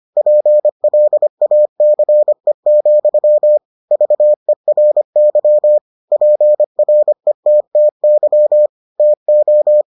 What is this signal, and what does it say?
Morse code, heard as a keyed tone, about 25 wpm, 600 Hz. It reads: PLACE, VERY PRETTY TO